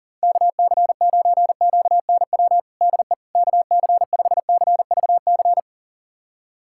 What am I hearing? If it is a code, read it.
Morse code, heard as a keyed tone, about 37 wpm, 700 Hz. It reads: KC9QNW DE KC5CVC
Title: USA Callsign DE Callsign 40wpm